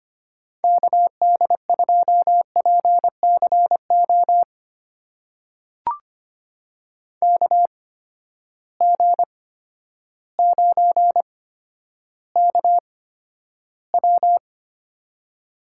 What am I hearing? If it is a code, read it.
Morse code, heard as a keyed tone, about 25 wpm, spaced as for 11 wpm, 700 Hz. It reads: KD2PCO KG9KW
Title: USA Call Signs - 8x Character Spacing - 25wpm